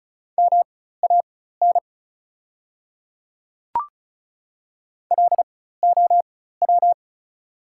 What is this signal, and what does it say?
Morse code, heard as a keyed tone, about 36 wpm, spaced as for 16 wpm, 690 Hz. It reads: MAN LOW
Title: 4x Character Spacing - Top 200 Words - 35wpm